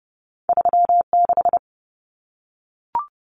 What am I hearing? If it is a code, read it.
Morse code, heard as a keyed tone, about 30 wpm, 700 Hz. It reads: 36
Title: Common Cut Numbers - 2-Digit 30wpm